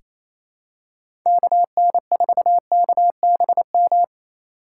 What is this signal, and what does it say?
Morse code, heard as a keyed tone, about 28 wpm, 700 Hz. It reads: KN4KBM
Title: USA Call Signs - 6 Characters - 3x Character Spacing - 28wpm